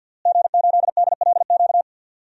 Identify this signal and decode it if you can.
Morse code, heard as a keyed tone, about 52 wpm, 700 Hz. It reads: G8DLQ
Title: DXCC-100 Call Signs - 5 Characters - Repeated 2x - 50wpm